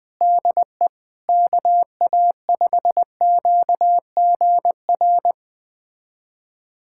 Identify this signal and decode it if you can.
Morse code, heard as a keyed tone, about 21 wpm, 700 Hz. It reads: DE KA5QGR
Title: USA Callsign DE Callsign 20wpm